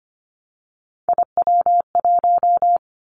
Morse code, heard as a keyed tone, about 25 wpm, 700 Hz. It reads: IW1
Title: Callsign Prefixes for Top 25 DXCC - 25wpm